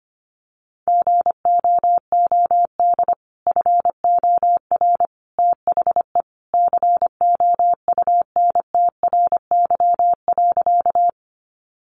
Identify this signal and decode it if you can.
Morse code, heard as a keyed tone, about 25 wpm, 700 Hz. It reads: GOOD FOR THE COUNTRY.
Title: Sentences from Top 300 Words 25wpm